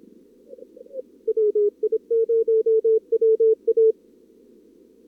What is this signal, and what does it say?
Morse code, heard as a keyed tone, about 27 wpm, 440 Hz. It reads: WI0WA
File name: CWSprint / Feb2022 / k5tr-feb-na-sprint-cw-k5ot -0100